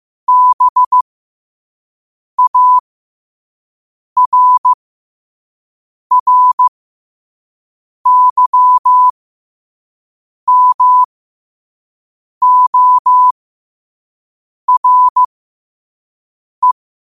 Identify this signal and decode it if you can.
Morse code, heard as a keyed tone, about 15 wpm, spaced as for 5 wpm, 1000 Hz. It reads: BARRYMORE
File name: qotdfiles / 5wpm